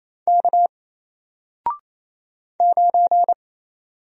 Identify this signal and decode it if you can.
Morse code, heard as a keyed tone, about 27 wpm, 700 Hz. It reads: K 9
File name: Single-Letter-Number-28wpm